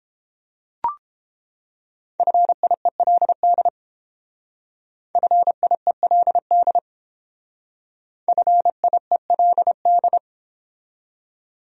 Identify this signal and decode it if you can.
Morse code, heard as a keyed tone, about 31 wpm, 700 Hz. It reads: FIELD FIELD FIELD